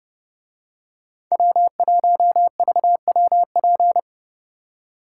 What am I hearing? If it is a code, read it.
Morse code, heard as a keyed tone, about 29 wpm, 700 Hz. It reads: W1VWP